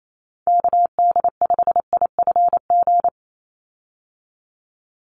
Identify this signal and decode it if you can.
Morse code, heard as a keyed tone, about 28 wpm, 700 Hz. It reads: KD5IFG
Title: USA Call Signs 28wpm